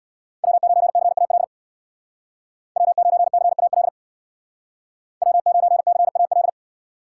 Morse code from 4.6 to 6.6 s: W9CAL

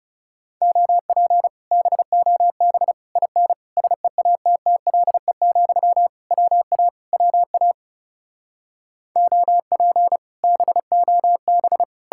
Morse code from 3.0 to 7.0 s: IN SEATTLE, WA

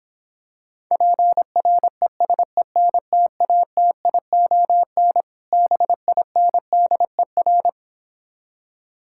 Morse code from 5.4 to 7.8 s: BINDER

